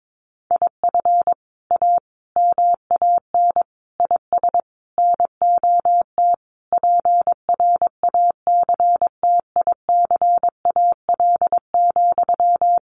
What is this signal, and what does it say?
Morse code, heard as a keyed tone, about 22 wpm, 700 Hz. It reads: IF A MAN IS NOT PRACTICAL,
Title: Sentences from Top 1400 Words 22wpm